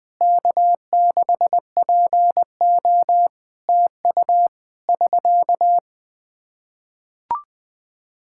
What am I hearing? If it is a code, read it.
Morse code, heard as a keyed tone, about 20 wpm, 700 Hz. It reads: K6PO TU <SK>
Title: Practice QSO - Exchange 4 - Ending - 20wpm